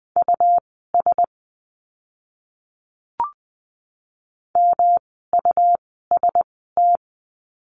Morse code, heard as a keyed tone, about 20 wpm, spaced as for 15 wpm, 700 Hz. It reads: US MUST
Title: 237 Lesson 38f - Letter q - Words 20wpm